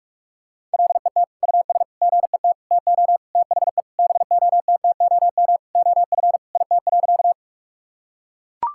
Morse code, heard as a keyed tone, about 41 wpm, 700 Hz. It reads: LET US GET TO THE BOTTOM OF IT.
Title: Sentences from Top 1000 Words - Review Part 6 - 45wpm